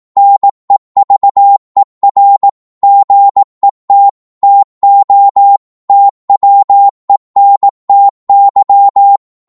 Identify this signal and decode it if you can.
Morse code, heard as a keyed tone, about 18 wpm, 800 Hz. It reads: NEVER GET TO TWENTY